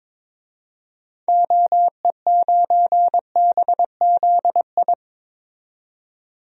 Morse code, heard as a keyed tone, about 22 wpm, 700 Hz. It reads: OE9BZI
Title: DXCC-100 Call Signs - 6 Characters - 22wpm